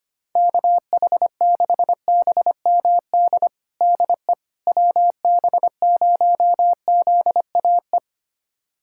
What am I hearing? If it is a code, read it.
Morse code, heard as a keyed tone, about 25 wpm, 700 Hz. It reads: KH6BMD DE WB0ZAE
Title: USA Callsign DE Callsign - 6 Characters - 25wpm